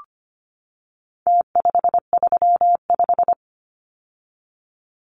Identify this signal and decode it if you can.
Morse code, heard as a keyed tone, about 25 wpm, 700 Hz. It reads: T535
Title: Numbers - Contest Style - 4-Digit 25wpm